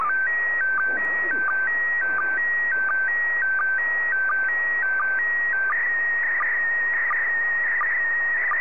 SSTV (slow-scan television), PD 90.
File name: sstv / PD90